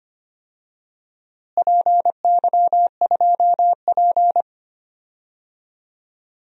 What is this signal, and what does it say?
Morse code, heard as a keyed tone, about 25 wpm, 700 Hz. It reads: PY2P
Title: DXCC-100 Call Signs - 4 Characters - Repeated 3x - 25wpm